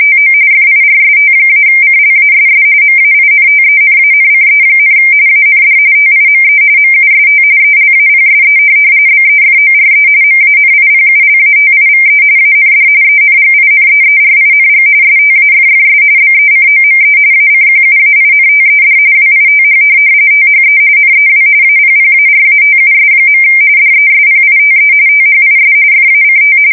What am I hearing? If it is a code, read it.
RTTY, about 45.45 baud. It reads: TO FIRST-CLASS MAINTENANCE, AND TIME MUST ELAPSE BEFORE MANIPULATIVE AND ENGINEERING STAFFS BECOME ACQUAINTED WITH THE NEW APPARATUS. DESPITE THIS FACT
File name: fsk170 / morkrum